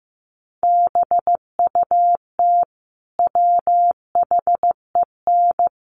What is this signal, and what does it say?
Morse code, heard as a keyed tone, about 15 wpm, 700 Hz. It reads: BUT WHEN
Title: Sentences from Top 100 Words 15wpm